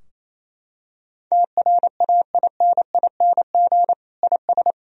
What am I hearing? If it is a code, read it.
Morse code, heard as a keyed tone, about 28 wpm, 700 Hz. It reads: TRAINING IS
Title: Sentences from Top 1000 Words - Review Part 4 - 28wpm